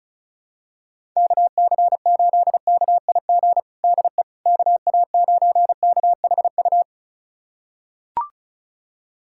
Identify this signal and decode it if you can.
Morse code, heard as a keyed tone, about 36 wpm, 700 Hz. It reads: KC8KIG DE KA9KHU
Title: USA Callsign DE Callsign - 6 Characters - 35wpm